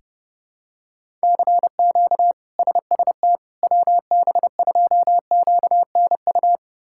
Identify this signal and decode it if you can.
Morse code, heard as a keyed tone, about 30 wpm, 700 Hz. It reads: CQ SST WB2QNU
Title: K1USN SST QSO Practice - 30wpm